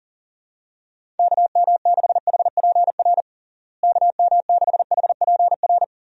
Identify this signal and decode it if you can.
Morse code, heard as a keyed tone, about 40 wpm, 700 Hz. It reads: KM6HPR KM6HPR
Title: USA Call Signs - 6 Characters - Repeated 3x - 40wpm